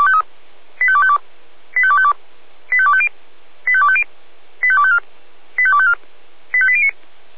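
Five-tone selective call (ZVEI), heard as quick runs of five tones, tones of about 70 ms. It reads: 85252 85252 85250 85250 85254 85254 85809
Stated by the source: ZVEI five-tone call 85259 received on an SDR, from feuerwehr-orlamuende.org fire brigade audio server.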